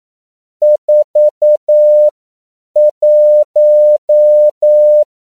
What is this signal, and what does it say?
Morse code, heard as a keyed tone, about 9 wpm, spaced as for 7 wpm, 600 Hz. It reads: 41
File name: CW / 10PPM